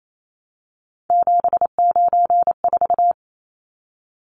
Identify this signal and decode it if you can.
Morse code, heard as a keyed tone, about 28 wpm, 700 Hz. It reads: 794
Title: Numbers - 3-Digit 28wpm